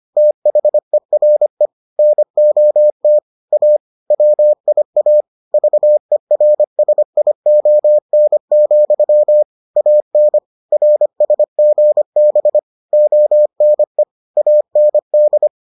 Morse code, heard as a keyed tone, about 25 wpm, 600 Hz. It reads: THERE NOT A WIA VERSION, AN RSGB ONE AND